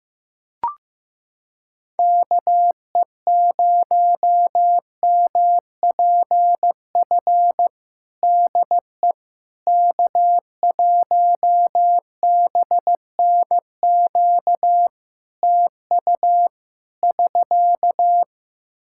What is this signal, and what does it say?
Morse code, heard as a keyed tone, about 15 wpm, 700 Hz. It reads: KE0MPF DE K1BNQ TU <SK>
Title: Practice QSO - Exchange 4 - Ending - 15wpm